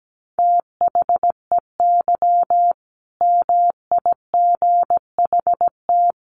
Morse code, heard as a keyed tone, about 17 wpm, 700 Hz. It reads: THEY MIGHT
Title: A Sound Of Thunder 17wpm